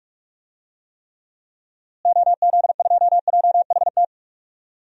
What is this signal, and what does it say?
Morse code, heard as a keyed tone, about 46 wpm, 700 Hz. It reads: OZ2JHT
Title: DXCC-100 Call Signs - 6 Characters - Repeated 2x - 45wpm